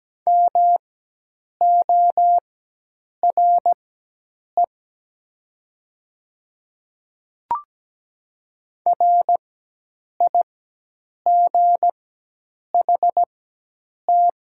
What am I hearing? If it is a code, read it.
Morse code, heard as a keyed tone, about 17 wpm, spaced as for 8 wpm, 700 Hz. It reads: MORE RIGHT
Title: 4x Character Spacing - Top 300 Words - 17wpm